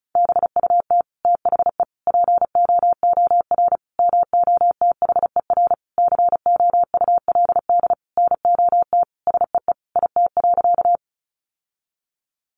Morse code, heard as a keyed tone, about 34 wpm, 700 Hz. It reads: BUT THE POOR MOTHER COULD NOT SEE IT.
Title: Sentences from Top 1000 Words 35wpm